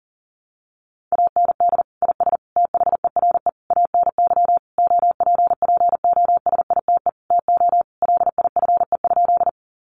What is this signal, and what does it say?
Morse code, heard as a keyed tone, about 40 wpm, 700 Hz. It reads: AND IS THERE ANY OPPOSITE TO LIFE?
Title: Sentences from Top 1300 Words 40wpm